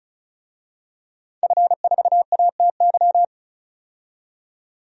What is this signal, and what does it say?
Morse code, heard as a keyed tone, about 36 wpm, 700 Hz. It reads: F4ATY
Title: DXCC-100 Call Signs - 5 Characters - 35wpm